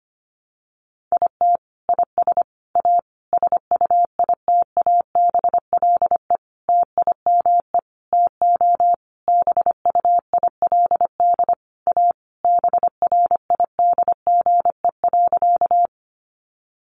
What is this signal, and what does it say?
Morse code, heard as a keyed tone, about 25 wpm, 700 Hz. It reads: IT IS A SUITABLE TIME TO BUILD A BRIDGE.
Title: Sentences from Top 1400 Words 25wpm